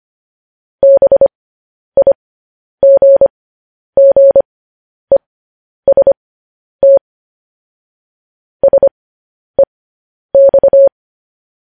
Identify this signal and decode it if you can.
Morse code, heard as a keyed tone, about 25 wpm, spaced as for 10 wpm, 550 Hz. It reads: BIGGEST SEX